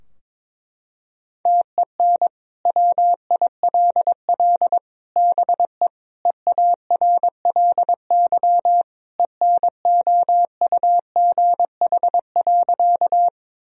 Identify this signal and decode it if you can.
Morse code, heard as a keyed tone, about 22 wpm, 700 Hz. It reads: TEN WILL BE EARLY ENOUGH.
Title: Sentences from Top 1000 Words - Review Part 2 - 22wpm